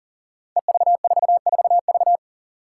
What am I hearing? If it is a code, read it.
Morse code, heard as a keyed tone, about 41 wpm, 700 Hz. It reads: EV44V
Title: Five Challenging Letter-Number Combinations 40wpm